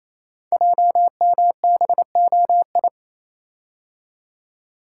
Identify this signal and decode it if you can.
Morse code, heard as a keyed tone, about 27 wpm, 700 Hz. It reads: JMBOI
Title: Code Groups - 5 Letters - 28wpm